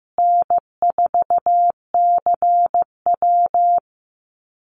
Morse code, heard as a keyed tone, about 15 wpm, 700 Hz. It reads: N4CW
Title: Call Signs Repeated 2x - 15wpm